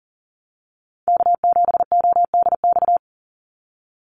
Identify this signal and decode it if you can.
Morse code, heard as a keyed tone, about 40 wpm, 700 Hz. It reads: K7ODX
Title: NA SOTA Activator Callsigns 40wpm